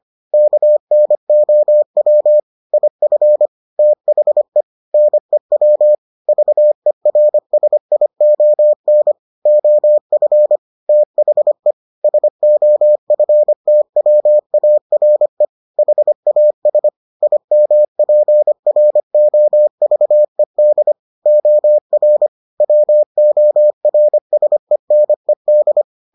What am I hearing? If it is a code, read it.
Morse code, heard as a keyed tone, about 25 wpm, 600 Hz. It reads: KNOW IF THE NEW VERSION OF THE SOFTWARE HAS IMPROVED OR WORSENED